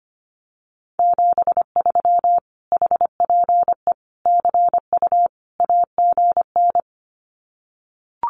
Morse code, heard as a keyed tone, about 25 wpm, 700 Hz. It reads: 73 HPE CU AGN
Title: Common QSO Phrases Encore 25wpm